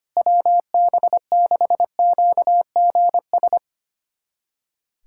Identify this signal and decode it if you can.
Morse code, heard as a keyed tone, about 25 wpm, 700 Hz. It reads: WB6QGS
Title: USA Callsign DE Callsign - 6 Characters - 25wpm